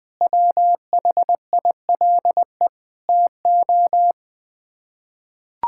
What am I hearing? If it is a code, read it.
Morse code, heard as a keyed tone, about 20 wpm, 700 Hz. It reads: WHILE TO
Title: Sets of 6 Words - 20wpm